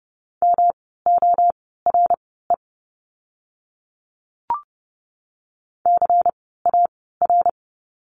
Morse code, heard as a keyed tone, about 30 wpm, spaced as for 17 wpm, 700 Hz. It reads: MORE CAR
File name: course / 30:10wpm / 108 Lesson 18f - Letter m - Words 30wpm